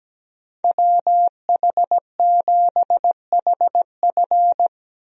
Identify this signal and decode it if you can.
Morse code, heard as a keyed tone, about 17 wpm, 700 Hz. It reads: WH7HF
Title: USA Call Signs - 5 Characters - Repeated 3x - 17wpm